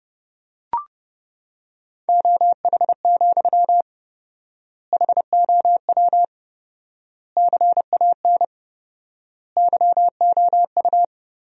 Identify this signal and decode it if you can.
Morse code, heard as a keyed tone, about 32 wpm, 700 Hz. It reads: OH, HOW CAN YOU
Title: 4x Word-Spacing - Sentences from Top 500 Words - 30wpm